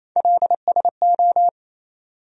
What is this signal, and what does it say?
Morse code, heard as a keyed tone, about 28 wpm, 700 Hz. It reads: LSO
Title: USA Call Sign Postfixes - 3 Letters 28wpm